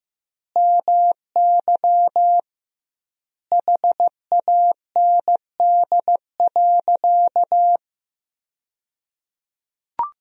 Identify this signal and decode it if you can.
Morse code, heard as a keyed tone, about 15 wpm, 700 Hz. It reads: MY HAND.
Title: 2x Word-Spacing - Sentences from Top 200 Words - 15wpm